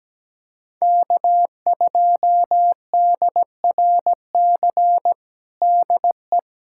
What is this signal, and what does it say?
Morse code, heard as a keyed tone, about 17 wpm, 700 Hz. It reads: K2DRC DE
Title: USA Callsign DE Callsign 17wpm